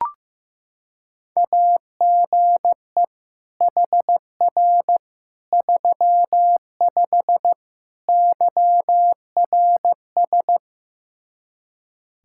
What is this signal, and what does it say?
Morse code, heard as a keyed tone, about 16 wpm, 700 Hz. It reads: AGE HR 35 YRS
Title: K1USN SST QSO Practice 15wpm